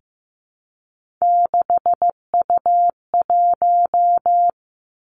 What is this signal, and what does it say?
Morse code, heard as a keyed tone, about 15 wpm, 700 Hz. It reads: 6U1
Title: Three Challenging Letter-Number Combinations 15wpm